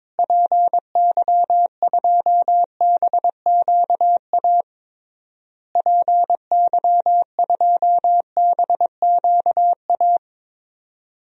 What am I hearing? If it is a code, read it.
Morse code, heard as a keyed tone, about 22 wpm, 700 Hz. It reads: PY2BQA PY2BQA
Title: DXCC-100 Call Signs - 6 Characters - Repeated 3x - 22wpm